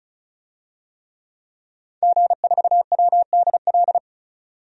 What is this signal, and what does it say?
Morse code, heard as a keyed tone, about 36 wpm, 700 Hz. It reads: G4WDL
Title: DXCC-100 Call Signs - 5 Characters - 35wpm